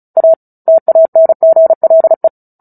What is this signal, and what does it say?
Morse code, heard as a keyed tone, about 34 wpm, 650 Hz. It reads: A TANGLE
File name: Book Two / 09-Chapter 9 - 35 WPM